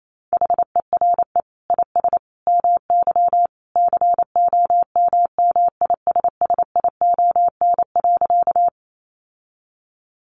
Morse code, heard as a keyed tone, about 28 wpm, 700 Hz. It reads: HERE IS MY COMMISSION.
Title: Sentences from Top 1400 Words 28wpm